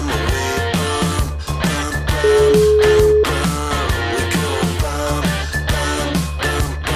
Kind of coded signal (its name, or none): telephone call progress tone